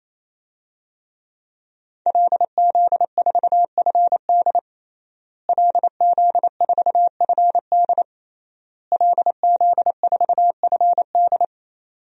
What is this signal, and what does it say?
Morse code, heard as a keyed tone, about 27 wpm, 700 Hz. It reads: LZ4FD LZ4FD LZ4FD
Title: DXCC-100 Call Signs - 5 Characters - Repeated 3x - 28wpm